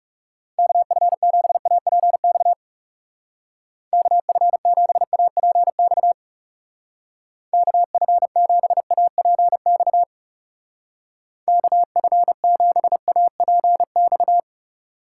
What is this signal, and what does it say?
Morse code, heard as a keyed tone, about 38 wpm, 700 Hz. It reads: KF7APX KF7APX KF7APX KF7APX